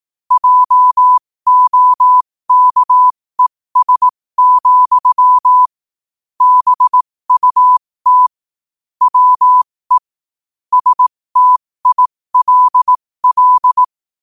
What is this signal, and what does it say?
Morse code, heard as a keyed tone, about 18 wpm, spaced as for 15 wpm, 1000 Hz. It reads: JOKES, BUT WE STILL
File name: qotdfiles / 15wpm